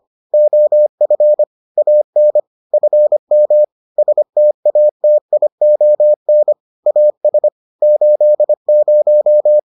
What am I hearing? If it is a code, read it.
Morse code, heard as a keyed tone, about 25 wpm, 600 Hz. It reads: OF AN FM STATION AS 80